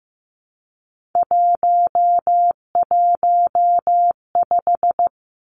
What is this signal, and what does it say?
Morse code, heard as a keyed tone, about 15 wpm, 700 Hz. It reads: 115
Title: Three Challenging Letter-Number Combinations 15wpm